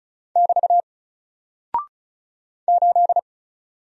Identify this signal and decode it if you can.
Morse code, heard as a keyed tone, about 36 wpm, 700 Hz. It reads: <BT> 8
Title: Single Character (Letter:Number:Punctuation:Prosign) 35wpm